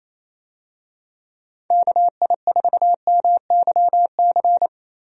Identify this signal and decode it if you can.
Morse code, heard as a keyed tone, about 28 wpm, 700 Hz. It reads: KI4MYC
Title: USA Call Signs - 6 Characters - Repeated 3x - 28wpm